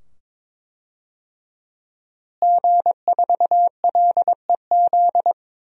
Morse code, heard as a keyed tone, about 21 wpm, 700 Hz. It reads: G4LEZ
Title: DXCC-100 Call Signs - 5 Characters - Repeated 2x - 22wpm